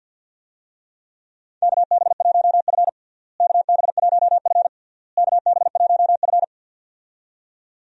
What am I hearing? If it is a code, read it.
Morse code, heard as a keyed tone, about 48 wpm, 700 Hz. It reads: KB1F KB1F KB1F
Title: USA Call Signs - 4 Characters - Repeated 3x - 50wpm